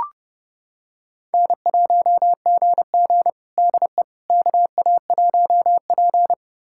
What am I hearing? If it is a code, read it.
Morse code, heard as a keyed tone, about 32 wpm, 700 Hz. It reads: N1GG DE KA1P
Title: USA Callsign DE Callsign - 4 Characters - 30wpm